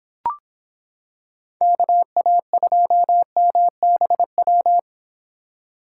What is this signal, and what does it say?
Morse code, heard as a keyed tone, about 26 wpm, 700 Hz. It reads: KA2MBW